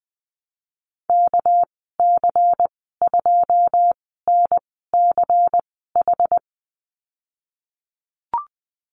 Morse code, heard as a keyed tone, about 20 wpm, spaced as for 15 wpm, 700 Hz. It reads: KC2NCH
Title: 200 Lesson 32g - Letter k - Call Signs 20wpm